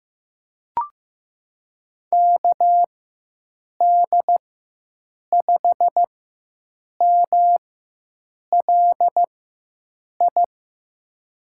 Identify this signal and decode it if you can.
Morse code, heard as a keyed tone, about 15 wpm, spaced as for 7 wpm, 700 Hz. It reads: KD5MLI
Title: USA Call Signs - 6 Characters - 4x Character Spacing - 15wpm